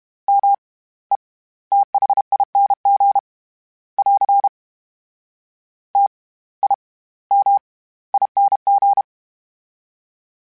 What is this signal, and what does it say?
Morse code, heard as a keyed tone, about 32 wpm, spaced as for 19 wpm, 800 Hz. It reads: METHING <AR> TIMING
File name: Special INGs / Special INGs-mp3s / SlowToFaster ing @30